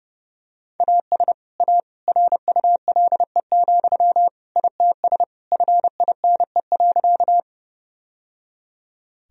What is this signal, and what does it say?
Morse code, heard as a keyed tone, about 32 wpm, 700 Hz. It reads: AS A RULE, ITS FINE.
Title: Sentences from Top 1000 Words - Review Part 2 - 30wpm